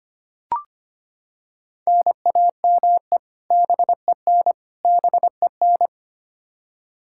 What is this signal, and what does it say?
Morse code, heard as a keyed tone, about 25 wpm, 700 Hz. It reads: NAME BEN BEN
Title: Practice QSO - Exchange 1 - Beginning - 25wpm